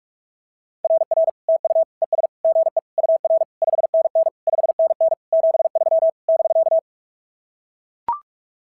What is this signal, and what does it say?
Morse code, heard as a keyed tone, about 43 wpm, 660 Hz. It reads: RR TU ES GE UR 5NN 5NN 73 <BK>
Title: SOTA QSO Practice 45wpm